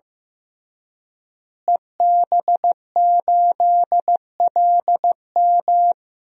Morse code, heard as a keyed tone, about 15 wpm, 700 Hz. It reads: EB8LM
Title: DXCC-100 Call Signs - 5 Characters - Repeated 3x - 15wpm